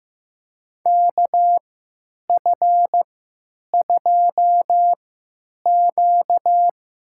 Morse code, heard as a keyed tone, about 15 wpm, spaced as for 9 wpm, 700 Hz. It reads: KF2Q
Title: USA Call Signs - 3x Character Spacing - 15wpm